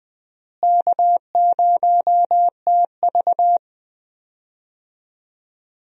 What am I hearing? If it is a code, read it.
Morse code, heard as a keyed tone, about 20 wpm, 700 Hz. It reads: K0TV
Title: USA Call Signs - 4 Characters - Repeated 3x - 20wpm